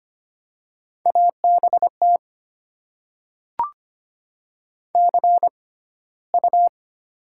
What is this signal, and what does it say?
Morse code, heard as a keyed tone, about 25 wpm, spaced as for 15 wpm, 700 Hz. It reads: ABT CU
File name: Top 90 QSO Elements - 6x Character Spacing - 25wpm